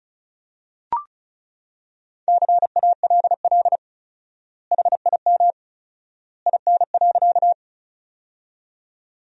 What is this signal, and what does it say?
Morse code, heard as a keyed tone, about 36 wpm, 700 Hz. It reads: CALL HIM IN.
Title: 4x Word-Spacing - Sentences from Top 100 Words - 35wpm